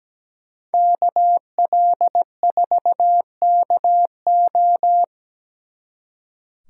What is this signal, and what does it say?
Morse code, heard as a keyed tone, about 21 wpm, 700 Hz. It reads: KL4KO
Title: Speed Racing - USA 2x2 Call Signs 17wpm